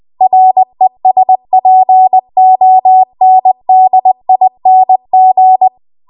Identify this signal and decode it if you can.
Morse code, heard as a keyed tone, about 20 wpm, 750 Hz. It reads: RESPONDING
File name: Archive / 20 WPM / 20WPM